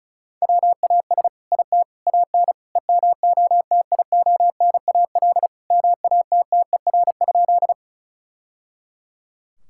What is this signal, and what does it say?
Morse code, heard as a keyed tone, about 36 wpm, 700 Hz. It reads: WAS IT AN EMOTIONAL MATTER?
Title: Sentences from Top 1000 Words - Review Part 5 - 35wpm